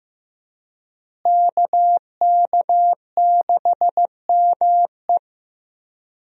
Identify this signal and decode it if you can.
Morse code, heard as a keyed tone, about 15 wpm, 700 Hz. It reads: KK6ME
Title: USA Call Signs - 5 Characters - Repeated 2x - 15wpm